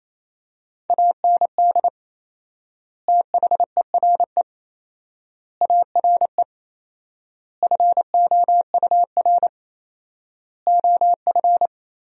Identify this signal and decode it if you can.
Morse code, heard as a keyed tone, about 28 wpm, 700 Hz. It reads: AND THERE ARE FOUR OF
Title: 4x Word-Spacing - Sentences from Top 300 Words - 28wpm